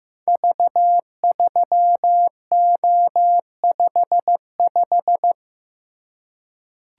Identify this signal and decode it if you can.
Morse code, heard as a keyed tone, about 15 wpm, 700 Hz. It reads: V3O55
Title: Code Groups - 5 Letter:Numbers - 15wpm